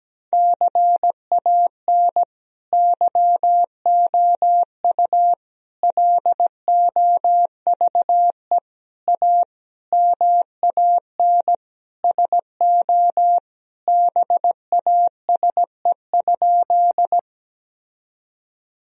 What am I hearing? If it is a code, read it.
Morse code, heard as a keyed tone, about 17 wpm, 700 Hz. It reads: CAN YOU LOVE A MAN SO BASE?
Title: Sentences from Top 1000 Words - Review Part 1 - 17wpm